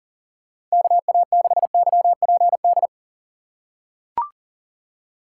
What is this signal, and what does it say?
Morse code, heard as a keyed tone, about 40 wpm, 700 Hz. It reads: KA6YPD
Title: USA Call Signs - 6 Characters - 2x Character Spacing - 40wpm